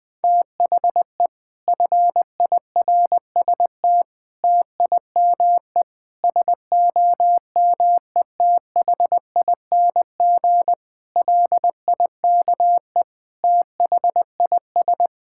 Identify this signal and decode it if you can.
Morse code, heard as a keyed tone, about 21 wpm, 700 Hz. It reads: THE FIRST TIME SOMETHING LIKE THIS
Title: Sets of 8 Words - 20wpm